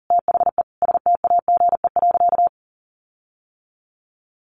Morse code, heard as a keyed tone, about 40 wpm, 700 Hz. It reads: THE STAGE.
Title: Sentences from Top 1400 Words 40wpm